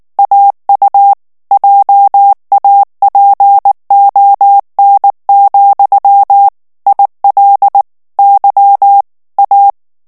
Morse code, heard as a keyed tone, about 19 wpm, 800 Hz. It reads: AU JAPON, IL Y A
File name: FRANCAIS TEXTE CLAIR 1 VITESSE 18 MOTS MINUTES